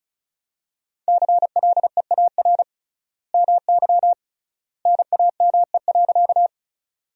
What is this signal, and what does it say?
Morse code, heard as a keyed tone, about 34 wpm, 700 Hz. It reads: CLEAR MY NAME.
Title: 3x Word-Spacing - Sentences from Top 500 Words - 35wpm